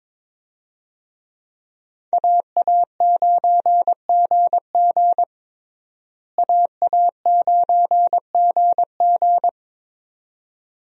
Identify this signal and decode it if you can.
Morse code, heard as a keyed tone, about 22 wpm, 700 Hz. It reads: AA9GG AA9GG
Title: USA Call Signs - 5 Characters - Repeated 2x - 22wpm